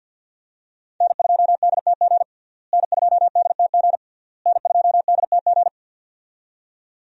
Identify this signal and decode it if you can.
Morse code, heard as a keyed tone, about 47 wpm, 700 Hz. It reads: N2DTG N2DTG N2DTG
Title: USA Call Signs - 5 Characters - Repeated 3x - 50wpm